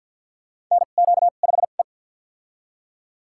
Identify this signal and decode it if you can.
Morse code, heard as a keyed tone, about 52 wpm, spaced as for 36 wpm, 700 Hz. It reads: NQ5E
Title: USA Call Signs - 2x Character Spacing - 50wpm